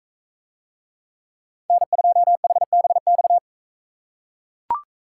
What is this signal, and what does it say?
Morse code, heard as a keyed tone, about 41 wpm, 700 Hz. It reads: N2HBX